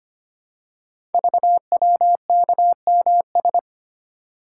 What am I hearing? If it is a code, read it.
Morse code, heard as a keyed tone, about 25 wpm, 700 Hz. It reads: VWKMS
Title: Code Groups - 5 Letters - 25wpm